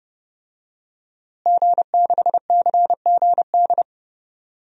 Morse code, heard as a keyed tone, about 31 wpm, 700 Hz. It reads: G6CGD
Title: DXCC-100 Call Signs - 5 Characters - Repeated 2x - 30wpm